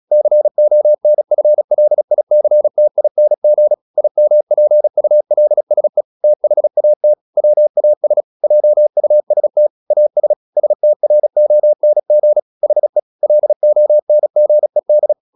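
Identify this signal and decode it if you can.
Morse code, heard as a keyed tone, about 36 wpm, 600 Hz. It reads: CONFLICTING IMPULSE THAT WAS JUST AS STRONG HE LONGED
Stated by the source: Morse practice at 36 wpm, text of Ken Follett's novel Never, from CW med Gnister.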